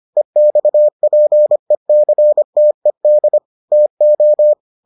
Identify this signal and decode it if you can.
Morse code, heard as a keyed tone, about 25 wpm, 600 Hz. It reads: EXPECTED TO